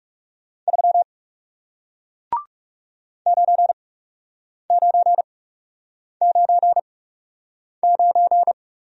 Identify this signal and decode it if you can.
Morse code, heard as a keyed tone, about 41 wpm, spaced as for 8 wpm, 700 Hz. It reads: 3 9999